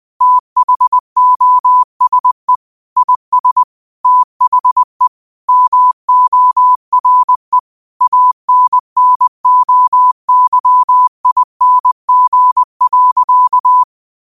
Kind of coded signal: Morse code